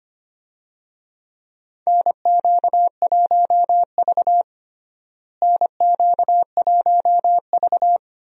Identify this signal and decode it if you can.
Morse code, heard as a keyed tone, about 25 wpm, 700 Hz. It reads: NQ1V NQ1V
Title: USA Call Signs - 4 Characters - Repeated 3x - 25wpm